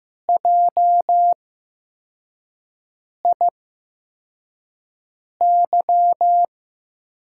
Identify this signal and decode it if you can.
Morse code, heard as a keyed tone, about 15 wpm, spaced as for 4 wpm, 700 Hz. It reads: JIY
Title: USA Call Signs - 6 Characters - 8x Character Spacing - 15wpm